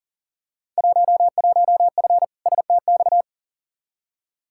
Morse code, heard as a keyed tone, about 40 wpm, 700 Hz. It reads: 11F STX
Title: Field Day Exchanges Encore - 40wpm